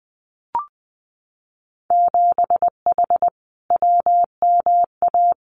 Morse code, heard as a keyed tone, about 19 wpm, 700 Hz. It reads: 7H WMA
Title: Winter Field Day Exchanges Encore 20wpm